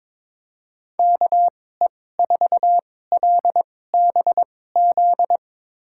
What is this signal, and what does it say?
Morse code, heard as a keyed tone, about 22 wpm, spaced as for 16 wpm, 700 Hz. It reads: KE4LBZ
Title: USA Call Signs - 6 Characters - 2x Character Spacing - 22wpm